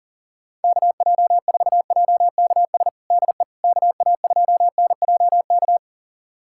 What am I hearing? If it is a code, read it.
Morse code, heard as a keyed tone, about 40 wpm, 700 Hz. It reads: KJ4JKS DE KA2NJK
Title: USA Callsign DE Callsign - 6 Characters - 40wpm